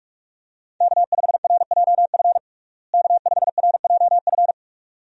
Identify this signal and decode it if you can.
Morse code, heard as a keyed tone, about 46 wpm, 700 Hz. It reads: K5RJF K5RJF
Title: USA Call Signs - 5 Characters - Repeated 2x - 45wpm